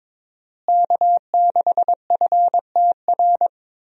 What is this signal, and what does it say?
Morse code, heard as a keyed tone, about 22 wpm, 700 Hz. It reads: K6FTR
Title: USA Call Signs - 5 Characters - Repeated 3x - 22wpm